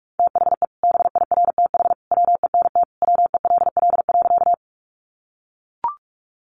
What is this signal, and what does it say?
Morse code, heard as a keyed tone, about 44 wpm, 700 Hz. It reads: THE BIRTH WENT WELL.
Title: Sentences from Top 800 Words 45wpm